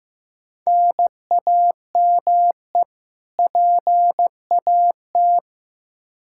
Morse code, heard as a keyed tone, about 15 wpm, 700 Hz. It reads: NAME PAT
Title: Practice QSO - Exchange 1 - Beginning - 15wpm